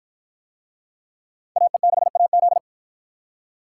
Morse code, heard as a keyed tone, about 53 wpm, 700 Hz. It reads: AE6AZ